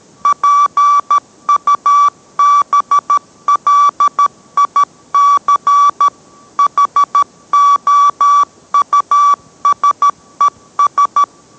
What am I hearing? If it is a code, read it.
Morse code, heard as a keyed tone, about 13 wpm, 1150 Hz. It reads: PUBLIC HOUSES